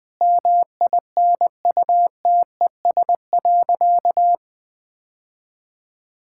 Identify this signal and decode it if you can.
Morse code, heard as a keyed tone, about 20 wpm, 700 Hz. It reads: MINUTES.
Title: Sentences with 3-Word Phrases - 20wpm